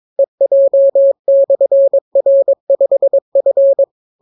Morse code, heard as a keyed tone, about 22 wpm, 550 Hz. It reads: EJ/R5F